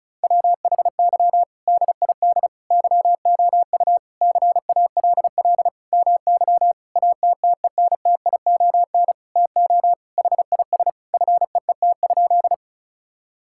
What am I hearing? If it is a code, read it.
Morse code, heard as a keyed tone, about 34 wpm, 700 Hz. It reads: WHY DID YOU CALL MY ATTENTION TO HIS FEET?
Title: Sentences from Top 1000 Words - Review Part 6 - 35wpm